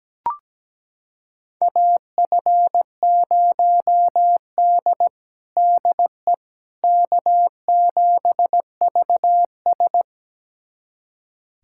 Morse code, heard as a keyed tone, about 17 wpm, 700 Hz. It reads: AF0D DE K7VS
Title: USA Callsign DE Callsign - 4 Characters - 17wpm